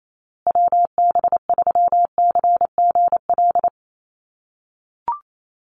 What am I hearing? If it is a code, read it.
Morse code, heard as a keyed tone, about 28 wpm, 700 Hz. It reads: WB3CGL
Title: USA Call Signs Repeated 2x 28wpm